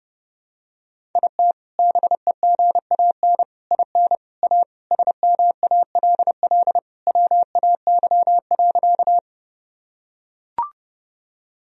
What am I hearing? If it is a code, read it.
Morse code, heard as a keyed tone, about 30 wpm, 700 Hz. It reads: IT BEGAN IN A SMALL WAY.
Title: Sentences from Top 1000 Words - Review Part 1 - 30wpm